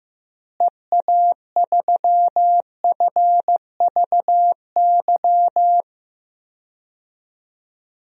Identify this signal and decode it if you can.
Morse code, heard as a keyed tone, about 15 wpm, 700 Hz. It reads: EA3FVY
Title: DXCC-100 Call Signs - 6 Characters - Repeated 3x - 15wpm